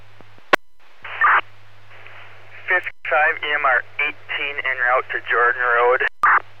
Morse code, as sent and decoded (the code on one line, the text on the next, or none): none